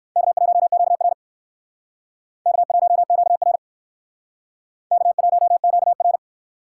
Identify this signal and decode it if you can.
Morse code, heard as a keyed tone, about 59 wpm, 700 Hz. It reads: K1QR K1QR K1QR